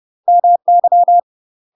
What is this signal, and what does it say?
Morse code, heard as a keyed tone, about 30 wpm, 700 Hz. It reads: MY
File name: UK MASTER QSOs TRAINING / 30WPM / UK MASTER QSOs 0013